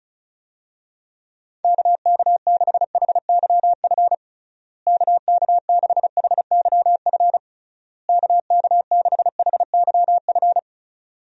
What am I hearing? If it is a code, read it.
Morse code, heard as a keyed tone, about 34 wpm, 700 Hz. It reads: KK6HYF KK6HYF KK6HYF
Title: USA Call Signs - 6 Characters - Repeated 3x - 35wpm